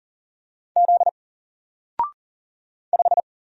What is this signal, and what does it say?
Morse code, heard as a keyed tone, about 39 wpm, 700 Hz. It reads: Z 5
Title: Single Character (Letter:Number:Punctuation:Prosign) 40wpm